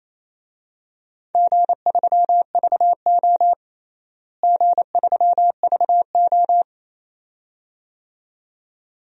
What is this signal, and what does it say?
Morse code, heard as a keyed tone, about 28 wpm, 700 Hz. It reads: G3VO G3VO
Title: DXCC-100 Call Signs - 4 Characters - Repeated 2x - 28wpm